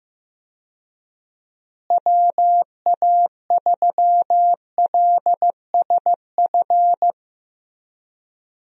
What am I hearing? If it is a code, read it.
Morse code, heard as a keyed tone, about 15 wpm, 700 Hz. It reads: WA3LSF
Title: USA Call Signs - 6 Characters - Repeated 2x - 15wpm